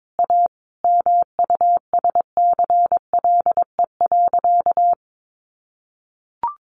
Morse code, heard as a keyed tone, about 22 wpm, 700 Hz. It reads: A MUSCLE.
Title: Sentences from Top 1300 Words 22wpm